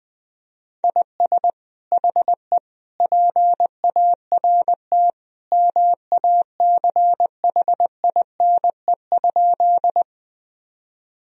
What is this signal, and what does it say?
Morse code, heard as a keyed tone, about 21 wpm, 700 Hz. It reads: IS HE PART MACHINE?
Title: Sentences from Top 1000 Words - Review Part 2 - 20wpm